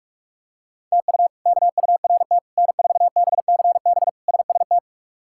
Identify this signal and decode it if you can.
Morse code, heard as a keyed tone, about 46 wpm, 700 Hz. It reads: TU KURT N4BCB SST
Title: K1USN SST QSO Practice - 45wpm